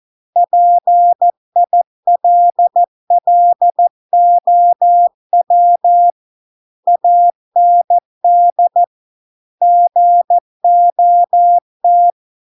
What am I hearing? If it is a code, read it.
Morse code, heard as a keyed tone, about 14 wpm, 700 Hz. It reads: PILLOW AND GOT